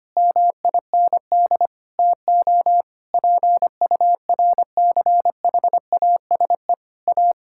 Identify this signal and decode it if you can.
Morse code, heard as a keyed tone, about 25 wpm, 700 Hz. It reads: MIND TO PURCHASE A
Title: Sentences from Top 1000 Words - Review Part 6 - 25wpm